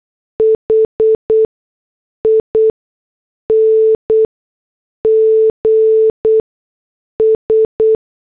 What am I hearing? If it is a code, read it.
Morse code, heard as a keyed tone, about 8 wpm, spaced as for 6 wpm, 440 Hz. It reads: HINGS